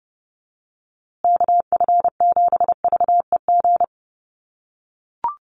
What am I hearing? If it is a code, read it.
Morse code, heard as a keyed tone, about 30 wpm, 700 Hz. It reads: KF7VEG
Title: USA Call Signs Repeated 2x 30wpm